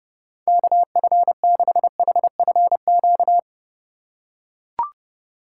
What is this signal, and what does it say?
Morse code, heard as a keyed tone, about 30 wpm, 700 Hz. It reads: KF6HFQ
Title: USA Call Signs - 6 Characters - 2x Character Spacing - 30wpm